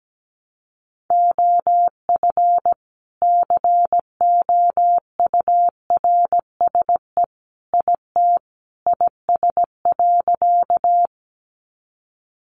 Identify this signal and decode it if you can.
Morse code, heard as a keyed tone, about 17 wpm, 700 Hz. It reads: OF COURSE IT IS.
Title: Sentences from Top 500 Words 17wpm